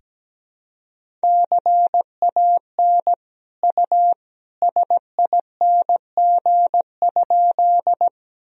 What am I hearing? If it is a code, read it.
Morse code, heard as a keyed tone, about 17 wpm, 700 Hz. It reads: CAN U SING?
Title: Sentences from Top 400 Words and Abbreviations 17wpm